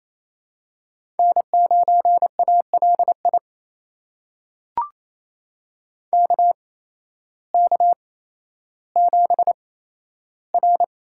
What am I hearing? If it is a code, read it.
Morse code, heard as a keyed tone, about 28 wpm, spaced as for 13 wpm, 700 Hz. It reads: N9ALI KK7R
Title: USA Call Signs - 8x Character Spacing - 28wpm